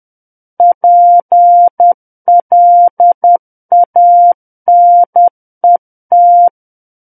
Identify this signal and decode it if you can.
Morse code, heard as a keyed tone, about 10 wpm, 700 Hz. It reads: PLANET
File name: Book One / 01-Chapter 1 - 10 WPM